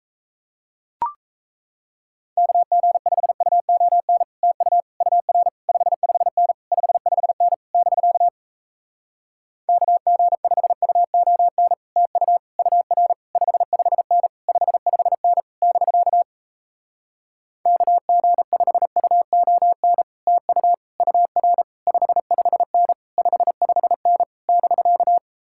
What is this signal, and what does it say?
Morse code, heard as a keyed tone, about 37 wpm, 700 Hz. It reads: KG5UON TU UR 55N 55N <BK> KG5UON TU UR 55N 55N <BK> KG5UON TU UR 55N 55N <BK>